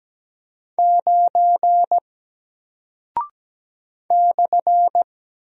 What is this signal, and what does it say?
Morse code, heard as a keyed tone, about 17 wpm, 700 Hz. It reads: 9 /
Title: Single Character (Letter:Number:Punctuation:Prosign) Alt 17wpm